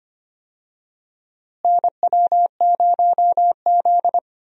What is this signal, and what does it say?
Morse code, heard as a keyed tone, about 25 wpm, 700 Hz. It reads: NW0Z
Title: USA Call Signs - 4 Characters - Repeated 3x - 25wpm